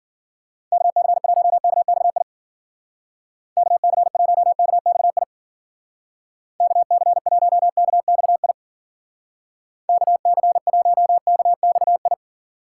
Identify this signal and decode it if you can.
Morse code, heard as a keyed tone, about 52 wpm, 700 Hz. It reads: KC1KXI KC1KXI KC1KXI KC1KXI